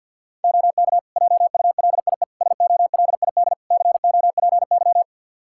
Morse code, heard as a keyed tone, about 47 wpm, 700 Hz. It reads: OK JULIE SOLID COPY
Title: Practice QSO - Exchange 3 - More Info - 50wpm